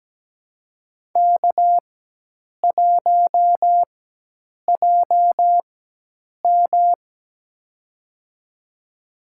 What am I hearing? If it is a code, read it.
Morse code, heard as a keyed tone, about 17 wpm, spaced as for 8 wpm, 700 Hz. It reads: K1JM
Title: USA Call Signs - 4 Characters - 4x Character Spacing - 17wpm